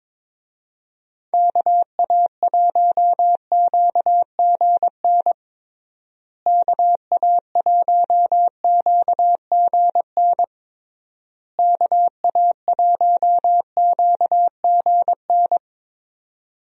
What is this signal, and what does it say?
Morse code, heard as a keyed tone, about 22 wpm, 700 Hz. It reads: KA1QGN KA1QGN KA1QGN
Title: USA Call Signs - 6 Characters - Repeated 3x - 22wpm